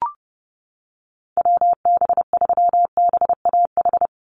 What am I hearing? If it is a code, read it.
Morse code, heard as a keyed tone, about 30 wpm, 700 Hz. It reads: WB3BAH
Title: USA Call Signs 30wpm